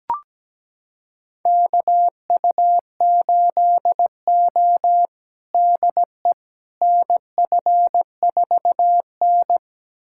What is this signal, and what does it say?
Morse code, heard as a keyed tone, about 17 wpm, 700 Hz. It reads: KU8O DE NF4N
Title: USA Callsign DE Callsign - 4 Characters - 17wpm